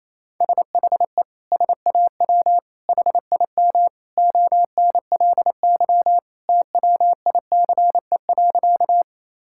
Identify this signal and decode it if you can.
Morse code, heard as a keyed tone, about 28 wpm, 700 Hz. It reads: SHE SAW HIM ONLY TWICE.
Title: Sentences from Top 1600 Words 28wpm